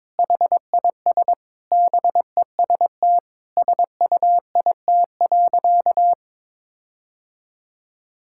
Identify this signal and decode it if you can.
Morse code, heard as a keyed tone, about 22 wpm, 700 Hz. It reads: HIS BEST SUIT.
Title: Sentences from Top 1600 Words 22wpm